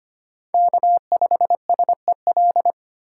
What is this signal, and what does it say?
Morse code, heard as a keyed tone, about 25 wpm, 700 Hz. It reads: K5SEL
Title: USA Call Signs - 3x Character Spacing - 25wpm